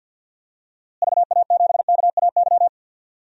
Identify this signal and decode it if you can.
Morse code, heard as a keyed tone, about 47 wpm, 700 Hz. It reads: VA7KAY
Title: DXCC-100 Call Signs - 6 Characters - Repeated 2x - 50wpm